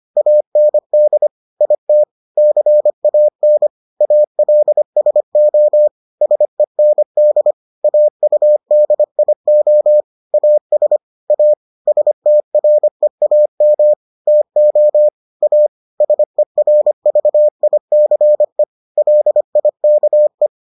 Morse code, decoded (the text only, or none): AND IT CAN ALSO SEND AUDIO AS A STREAM TO A SERVICE LIKE